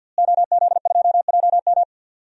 Morse code, heard as a keyed tone, about 47 wpm, 700 Hz. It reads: OZ2JM
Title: DXCC-100 Call Signs - 5 Characters - Repeated 3x - 50wpm